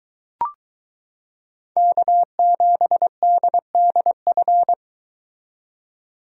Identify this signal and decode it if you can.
Morse code, heard as a keyed tone, about 22 wpm, 700 Hz. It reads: K7DDF